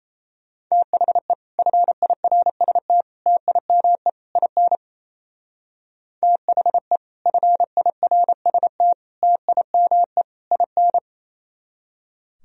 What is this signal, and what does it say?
Morse code, heard as a keyed tone, about 34 wpm, 700 Hz. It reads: THE FIRST TIME IN THE FIRST TIME IN
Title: Speed-Racing - Sets of 4 Words 28wpm